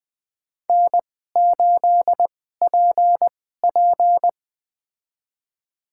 Morse code, heard as a keyed tone, about 20 wpm, spaced as for 14 wpm, 700 Hz. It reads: N8PP
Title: USA Call Signs - 2x Character Spacing - 20wpm